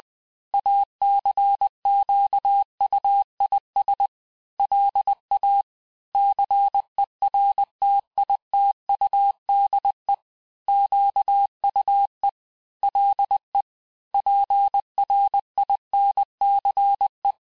Morse code, heard as a keyed tone, about 20 wpm, 800 Hz. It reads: ACQUIS LA CERTITUDE QUE LE PRINCE
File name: cours-du-mercredi / audios / cours-870-texte-CLAIR-vitesse-19-MPM